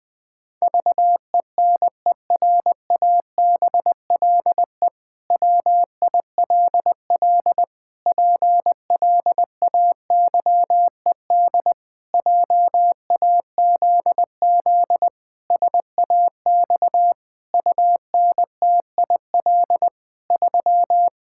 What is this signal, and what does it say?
Morse code, heard as a keyed tone, about 20 wpm, 700 Hz. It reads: VENERABLE WILL PLAYED JAZZ SAX UNTIL 3
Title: Pangram Challenge v2 - 20wpm